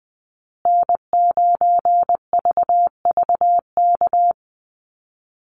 Morse code, heard as a keyed tone, about 20 wpm, 700 Hz. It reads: N9VVK
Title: USA Call Signs Repeated 2x 20wpm